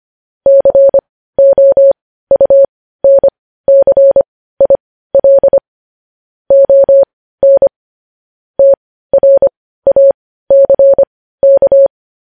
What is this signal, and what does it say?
Morse code, heard as a keyed tone, about 25 wpm, spaced as for 15 wpm, 550 Hz. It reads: COUNCIL ON TRACK
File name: podcast / News headlines-1x-25x15